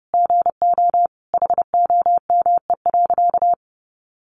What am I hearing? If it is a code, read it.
Morse code, heard as a keyed tone, about 30 wpm, 700 Hz. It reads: GO HOME.
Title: Sentences from Top 1400 Words 30wpm